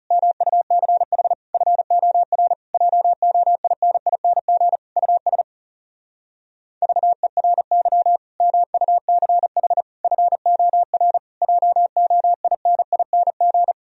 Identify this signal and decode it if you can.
Morse code, heard as a keyed tone, about 37 wpm, 700 Hz. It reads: MUCH FOR JOINING US VERY MUCH FOR JOINING
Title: Speed-Racing - Sets of 5 Words 30wpm